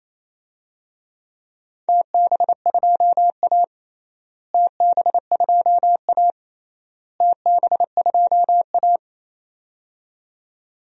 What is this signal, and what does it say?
Morse code, heard as a keyed tone, about 27 wpm, 700 Hz. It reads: TB2A TB2A TB2A
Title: DXCC-100 Call Signs - 4 Characters - Repeated 3x - 28wpm